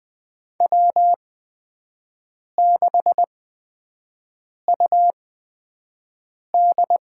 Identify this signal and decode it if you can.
Morse code, heard as a keyed tone, about 20 wpm, spaced as for 5 wpm, 700 Hz. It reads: W6UD
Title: USA Call Signs - 4 Characters - 8x Character Spacing - 20wpm